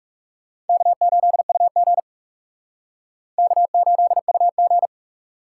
Morse code, heard as a keyed tone, about 37 wpm, 700 Hz. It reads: K8UG K8UG
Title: Speed Racing - USA 1x2 Call Signs 30wpm